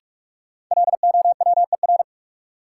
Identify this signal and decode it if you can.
Morse code, heard as a keyed tone, about 46 wpm, 700 Hz. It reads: LOWER
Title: Sentences from Top 700 Words - New Word Drill - 45wpm